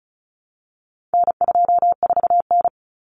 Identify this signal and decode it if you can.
Morse code, heard as a keyed tone, about 34 wpm, 700 Hz. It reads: N24N
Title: Common Cut Numbers - 4-Digit 35wpm